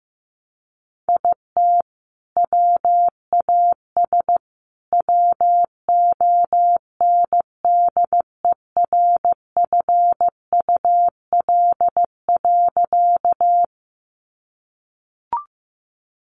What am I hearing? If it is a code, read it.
Morse code, heard as a keyed tone, about 15 wpm, 700 Hz. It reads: IT WAS WONDERFUL.
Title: Sentences from Top 1100 Words 15wpm